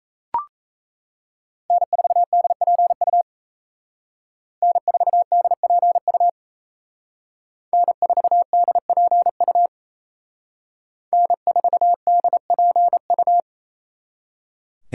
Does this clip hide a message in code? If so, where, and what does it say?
Yes: Morse code from 1.7 to 13.4 s, heard as a keyed tone, about 36 wpm, 700 Hz. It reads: N4DPU N4DPU N4DPU N4DPU